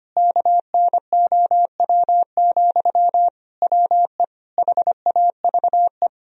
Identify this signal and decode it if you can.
Morse code, heard as a keyed tone, about 25 wpm, 700 Hz. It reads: KNOW, WE HAVE
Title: Sentences from Top 1000 Words - Review Part 3 - 25wpm